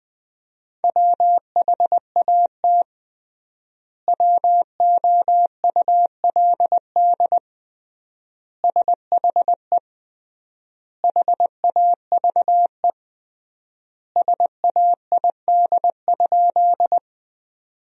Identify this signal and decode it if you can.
Morse code, heard as a keyed tone, about 20 wpm, 700 Hz. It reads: WHAT WOULD SHE HAVE SAID?
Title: 3x Word-Spacing - Sentences from Top 100 Words - 20wpm